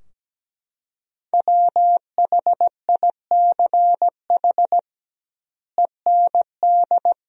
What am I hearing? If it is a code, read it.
Morse code, heard as a keyed tone, about 17 wpm, 700 Hz. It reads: WHICH END
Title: 2x Word-Spacing - Sentences from Top 300 Words - 17wpm